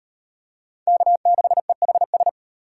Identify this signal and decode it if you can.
Morse code, heard as a keyed tone, about 38 wpm, 700 Hz. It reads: K6EHS